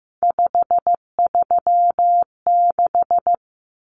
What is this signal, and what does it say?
Morse code, heard as a keyed tone, about 15 wpm, 700 Hz. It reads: 536
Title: Common Cut Numbers - 3-Digit 15wpm